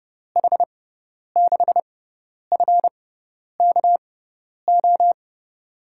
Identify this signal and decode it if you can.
Morse code, heard as a keyed tone, about 32 wpm, spaced as for 10 wpm, 700 Hz. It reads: H6FKO
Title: USA Call Signs - 6 Characters - 6x Character Spacing - 30wpm